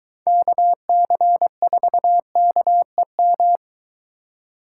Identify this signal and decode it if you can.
Morse code, heard as a keyed tone, about 22 wpm, 700 Hz. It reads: KC4KEM